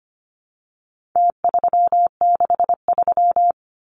Morse code, heard as a keyed tone, about 25 wpm, 700 Hz. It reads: T363
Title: Numbers - Contest Style - 4-Digit 25wpm